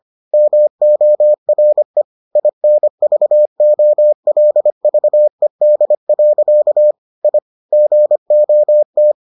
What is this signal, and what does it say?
Morse code, heard as a keyed tone, about 25 wpm, 600 Hz. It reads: MORE INVOLVED. I GOT